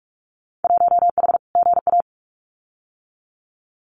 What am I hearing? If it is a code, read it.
Morse code, heard as a keyed tone, about 46 wpm, 700 Hz. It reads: 1H GA
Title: Winter Field Day Exchanges 45wpm